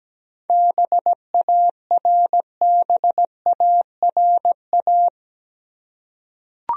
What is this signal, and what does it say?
Morse code, heard as a keyed tone, about 17 wpm, 700 Hz. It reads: BARBARA
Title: USA Common Names - 17wpm